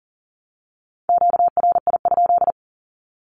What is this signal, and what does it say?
Morse code, heard as a keyed tone, about 40 wpm, 700 Hz. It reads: QRI?
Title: Q-Codes 40wpm